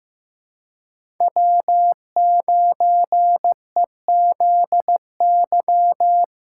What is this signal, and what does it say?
Morse code, heard as a keyed tone, about 15 wpm, 700 Hz. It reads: W9EZY
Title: USA Call Signs - 5 Characters - 4x Character Spacing - 15wpm